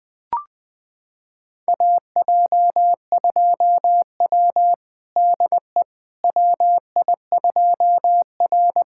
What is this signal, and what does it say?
Morse code, heard as a keyed tone, about 20 wpm, 700 Hz. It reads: AJ2W DE WI2R
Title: USA Callsign DE Callsign - 4 Characters - 20wpm